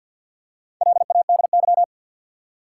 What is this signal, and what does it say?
Morse code, heard as a keyed tone, about 52 wpm, 700 Hz. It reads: LADY
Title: Four Letter Words - 50wpm